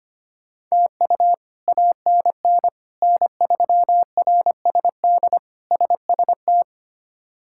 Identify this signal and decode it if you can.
Morse code, heard as a keyed tone, about 25 wpm, 700 Hz. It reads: TU ANN N3RSD SST
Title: K1USN SST QSO Practice - 25wpm